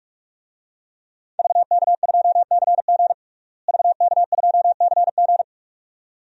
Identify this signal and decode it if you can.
Morse code, heard as a keyed tone, about 46 wpm, 700 Hz. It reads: VK2CG VK2CG
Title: DXCC-100 Call Signs - 5 Characters - Repeated 2x - 45wpm